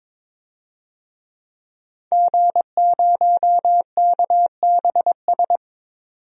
Morse code, heard as a keyed tone, about 22 wpm, 700 Hz. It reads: G0KBS
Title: DXCC-100 Call Signs - 5 Characters - 22wpm